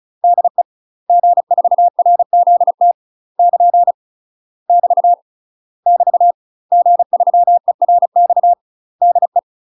Morse code, heard as a keyed tone, about 36 wpm, 700 Hz. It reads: DE G4RZT <KN> <BT> <BT> G3ERX DE
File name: UK MASTER QSOs TRAINING / 35WPM / UK MASTER QSOs 0026